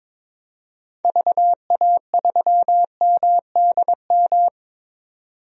Morse code, heard as a keyed tone, about 22 wpm, 700 Hz. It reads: VA3MDM
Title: DXCC-100 Call Signs - 6 Characters - 22wpm